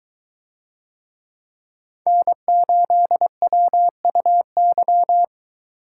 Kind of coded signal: Morse code

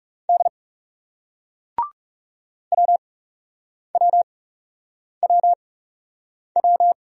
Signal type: Morse code

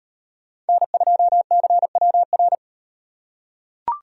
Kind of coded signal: Morse code

